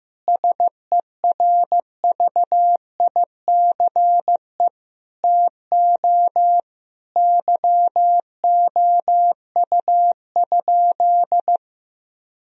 Morse code, heard as a keyed tone, about 15 wpm, 700 Hz. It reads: SERVICE TO YOU?
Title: Sentences from Top 1000 Words - Review Part 4 - 15wpm